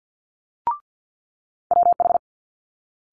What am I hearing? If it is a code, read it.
Morse code, heard as a keyed tone, about 50 wpm, 700 Hz. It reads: WH
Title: Multi-letter Phonemes Quick Fire 50wpm